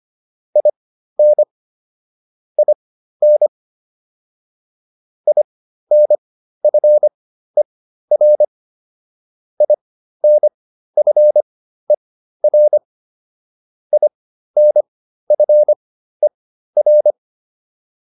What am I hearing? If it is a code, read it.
Morse code, heard as a keyed tone, about 25 wpm, spaced as for 13 wpm, 600 Hz. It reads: IN IN INFER INFER INFER